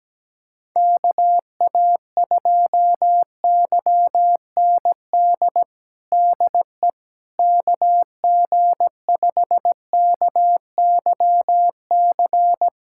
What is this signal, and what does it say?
Morse code, heard as a keyed tone, about 17 wpm, 700 Hz. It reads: KA2YND DE KG5KYC
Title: USA Callsign DE Callsign - 6 Characters - 17wpm